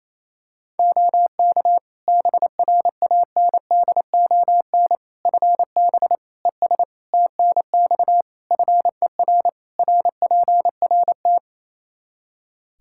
Morse code, heard as a keyed tone, about 28 wpm, 700 Hz. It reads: OK BRANDON FB ES TNX FER RPRT
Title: Practice QSO - Exchange 2 - Info - 28wpm